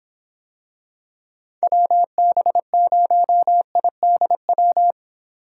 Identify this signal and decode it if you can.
Morse code, heard as a keyed tone, about 25 wpm, 700 Hz. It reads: WB0IDW